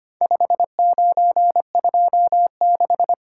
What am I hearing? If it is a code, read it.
Morse code, heard as a keyed tone, about 25 wpm, 700 Hz. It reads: 5926
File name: practice / 25wpm / Four Letter-Number Combinations - 25wpm